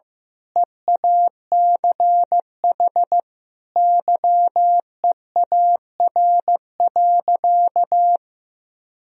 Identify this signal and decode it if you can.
Morse code, heard as a keyed tone, about 15 wpm, 700 Hz. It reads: EACH YEAR.
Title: Sentences from Top 1000 Words - Review Part 4 - 15wpm